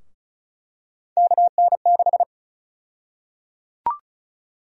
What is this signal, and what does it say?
Morse code, heard as a keyed tone, about 34 wpm, 700 Hz. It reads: KN6
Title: USA Call Sign Prefixes - 35wpm